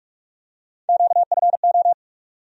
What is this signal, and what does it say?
Morse code, heard as a keyed tone, about 42 wpm, 700 Hz. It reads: QFO
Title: USA Call Sign Postfixes - 3 Letters 45wpm